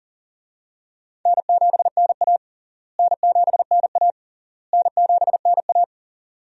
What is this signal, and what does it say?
Morse code, heard as a keyed tone, about 37 wpm, 700 Hz. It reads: N7NA N7NA N7NA
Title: USA Call Signs - 4 Characters - Repeated 3x - 40wpm